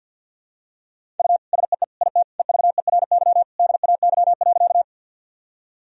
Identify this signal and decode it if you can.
Morse code, heard as a keyed tone, about 50 wpm, 700 Hz. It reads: U SEE IT EVERY DAY.
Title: Sentences from Top 200 Words and Abbreviations 50wpm